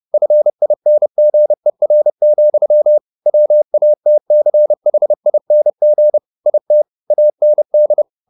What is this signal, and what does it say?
Morse code, heard as a keyed tone, about 30 wpm, 600 Hz. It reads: FINGER, WATCHING IT AND
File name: the old man and the sea / the old man and the sea 1 30wpm